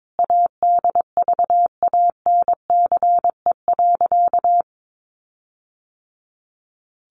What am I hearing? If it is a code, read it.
Morse code, heard as a keyed tone, about 22 wpm, 700 Hz. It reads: ADVANCE.
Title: Sentences from Top 1400 Words 22wpm